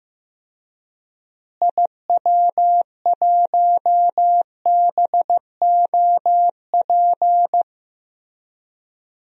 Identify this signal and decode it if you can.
Morse code, heard as a keyed tone, about 15 wpm, 700 Hz. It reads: IW1BOP
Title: DXCC-100 Call Signs - 6 Characters - Repeated 2x - 15wpm